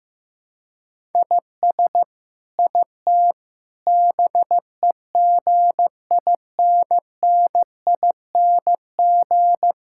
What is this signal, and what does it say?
Morse code, heard as a keyed tone, about 15 wpm, 700 Hz. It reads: IS IT BEGINNING
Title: Sentences from Top 1000 Words - Review Part 5 - 15wpm